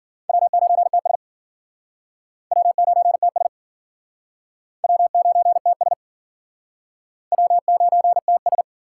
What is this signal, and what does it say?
Morse code, heard as a keyed tone, about 51 wpm, 700 Hz. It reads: W9TS W9TS W9TS W9TS